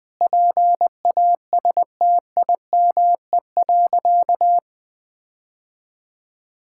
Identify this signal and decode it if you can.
Morse code, heard as a keyed tone, about 20 wpm, 700 Hz. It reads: PASTIME.
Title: Difficult Sentences with Hints - 20wpm